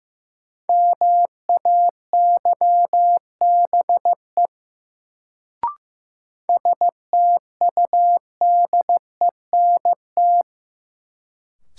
Morse code, heard as a keyed tone, about 15 wpm, 700 Hz. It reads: MAYBE STUDENT
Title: Sentences from Top 1000 Words - New Word Drill - 15wpm